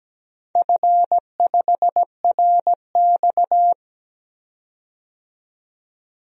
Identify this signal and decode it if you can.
Morse code, heard as a keyed tone, about 17 wpm, 700 Hz. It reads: F5RX
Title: DXCC-100 Call Signs - 4 Characters - 17wpm